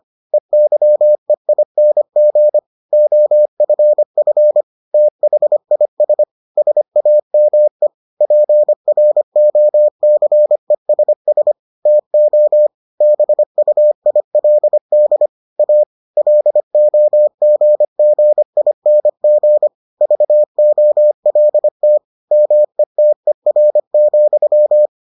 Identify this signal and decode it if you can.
Morse code, heard as a keyed tone, about 25 wpm, 600 Hz. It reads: EYEING OFF THIS SAME PROCESS TO BUILD A LOGGING VOLT METER,